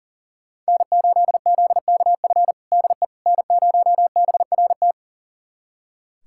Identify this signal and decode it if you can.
Morse code, heard as a keyed tone, about 37 wpm, 700 Hz. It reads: N8ZKF DE N0BRT
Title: USA Callsign DE Callsign - 5 Characters - 40wpm